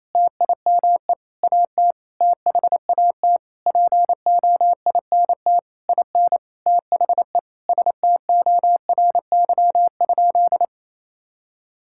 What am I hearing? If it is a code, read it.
Morse code, heard as a keyed tone, about 27 wpm, 700 Hz. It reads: TIME AT THAT POINT IN THE STORY?
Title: Sentences from Top 1000 Words - Review Part 5 - 28wpm